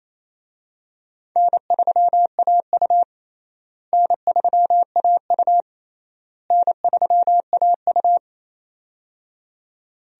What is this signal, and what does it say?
Morse code, heard as a keyed tone, about 28 wpm, 700 Hz. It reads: N3AU N3AU N3AU
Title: USA Call Signs - 4 Characters - Repeated 3x - 28wpm